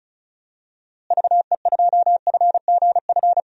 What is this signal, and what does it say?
Morse code, heard as a keyed tone, about 36 wpm, 700 Hz. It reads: VE2FGF
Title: DXCC-100 Call Signs - 6 Characters - Repeated 2x - 35wpm